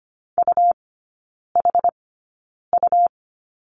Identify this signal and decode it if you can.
Morse code, heard as a keyed tone, about 25 wpm, spaced as for 9 wpm, 700 Hz. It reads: UHU
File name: Lesson 17e - Letter u - Three Characters 25wpm